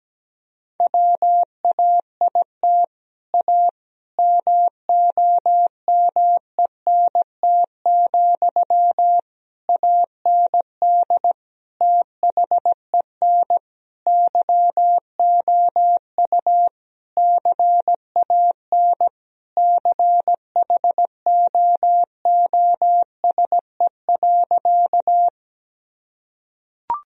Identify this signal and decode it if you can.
Morse code, heard as a keyed tone, about 17 wpm, 700 Hz. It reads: WAIT A MOMENT, AND THEN YOU CAN CHOOSE.
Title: Sentences from Top 1000 Words - Review Part 4 - 17wpm